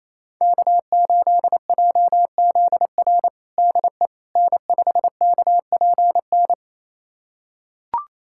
Morse code, heard as a keyed tone, about 28 wpm, 700 Hz. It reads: K8JZR DE N5KPN
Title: USA Callsign DE Callsign - 5 Characters - 28wpm